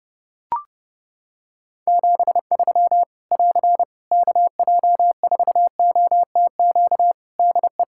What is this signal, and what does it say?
Morse code, heard as a keyed tone, about 31 wpm, 700 Hz. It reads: 73 <AR> KJ4OTQ DE
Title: Practice QSO - Exchange 4 - Ending - 30wpm